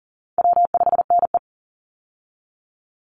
Morse code, heard as a keyed tone, about 38 wpm, 700 Hz. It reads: W5NE
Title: Call Signs Repeated 2x - 40wpm